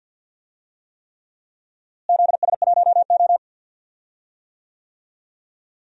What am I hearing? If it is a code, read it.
Morse code, heard as a keyed tone, about 49 wpm, 700 Hz. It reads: ZS1O
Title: DXCC-100 Call Signs - 4 Characters - Repeated 3x - 50wpm